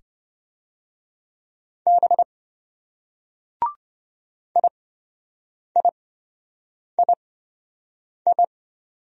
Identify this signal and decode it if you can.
Morse code, heard as a keyed tone, about 25 wpm, spaced as for 7 wpm, 700 Hz. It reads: B IIII